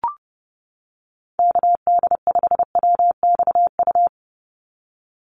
Morse code, heard as a keyed tone, about 30 wpm, 700 Hz. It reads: KD5WXU
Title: USA Call Signs 30wpm